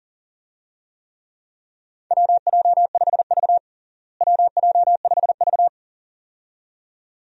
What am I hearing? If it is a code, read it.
Morse code, heard as a keyed tone, about 40 wpm, 700 Hz. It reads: WJ5V WJ5V
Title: USA Call Signs - 4 Characters - Repeated 2x - 40wpm